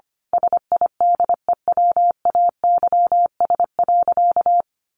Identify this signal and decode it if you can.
Morse code, heard as a keyed tone, about 25 wpm, 700 Hz. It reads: SIDEWAYS.
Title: A Sound Of Thunder 25wpm